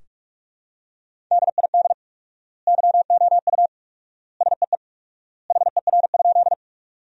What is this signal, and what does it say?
Morse code, heard as a keyed tone, about 46 wpm, 700 Hz. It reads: DID YOU SEE HER?
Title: 4x Word-Spacing - Sentences from Top 100 Words - 45wpm